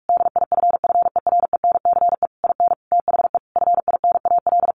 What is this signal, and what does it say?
Morse code, heard as a keyed tone, about 44 wpm, 700 Hz. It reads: DIFFERENCE IN THE FINAL
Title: Sentences from Top 800 Words 45wpm